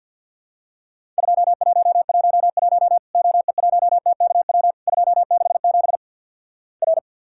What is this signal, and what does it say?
Morse code, heard as a keyed tone, about 50 wpm, 700 Hz. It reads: 2111 OE1TKW 267 R
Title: Worked All Europe DX Contest - QTC Exchange - 50wpm